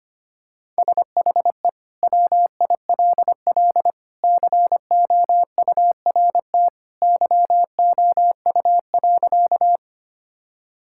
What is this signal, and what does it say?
Morse code, heard as a keyed tone, about 25 wpm, 700 Hz. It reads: SHE WILL COURT YOU.
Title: Sentences from Top 1600 Words 25wpm